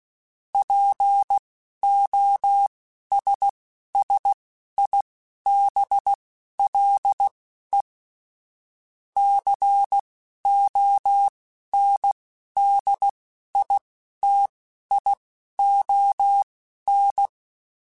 Morse code, heard as a keyed tone, about 16 wpm, spaced as for 11 wpm, 800 Hz. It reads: POSSIBLE CONDITION
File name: Technical Subjects / Technical Subjects-mp3s / AND Gate @10